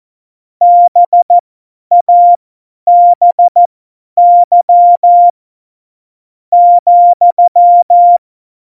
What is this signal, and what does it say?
Morse code, heard as a keyed tone, about 14 wpm, spaced as for 8 wpm, 700 Hz. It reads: BABY,
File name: Morsekurs / Witze / Witz 18. September 2018, 50 F70